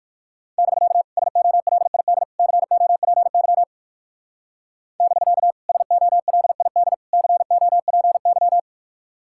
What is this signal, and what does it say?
Morse code, heard as a keyed tone, about 55 wpm, 700 Hz. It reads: <BK> SOLID COPY <BK> SOLID COPY
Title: Speed-Racing Common QSO Phrases - 45wpm